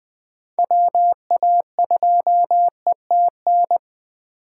Morse code, heard as a keyed tone, about 20 wpm, 700 Hz. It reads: WA2ETN